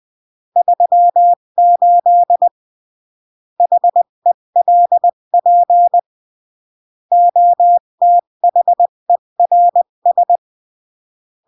Morse code, heard as a keyed tone, about 20 wpm, 700 Hz. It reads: 38 HELP OTHERS
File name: CW / jodorowsky / 82 commandments 20-15